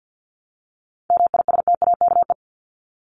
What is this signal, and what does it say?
Morse code, heard as a keyed tone, about 48 wpm, 700 Hz. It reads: MISTAKE
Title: Top 1501-1750 Words 50wpm